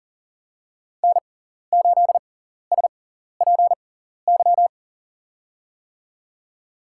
Morse code, heard as a keyed tone, about 40 wpm, spaced as for 14 wpm, 700 Hz. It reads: N8SPY
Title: USA Call Signs - 5 Characters - 6x Character Spacing - 40wpm